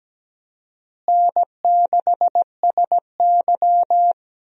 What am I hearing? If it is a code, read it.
Morse code, heard as a keyed tone, about 17 wpm, 700 Hz. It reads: N6SY
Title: USA Call Signs - 4 Characters - Repeated 3x - 17wpm